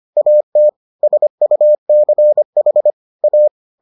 Morse code, heard as a keyed tone, about 25 wpm, 600 Hz. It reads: AT SUCH A